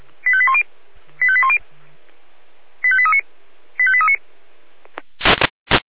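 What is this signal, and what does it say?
Five-tone selective call (ZVEI), heard as quick runs of five tones, tones of about 70 ms. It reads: 85810 85810 85839 85839
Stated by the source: ZVEI five-tone selective call received via SDR.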